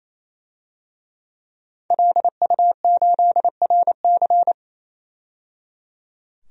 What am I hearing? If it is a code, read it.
Morse code, heard as a keyed tone, about 28 wpm, 700 Hz. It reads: LU8RC
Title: DXCC-100 Call Signs - 5 Characters - 28wpm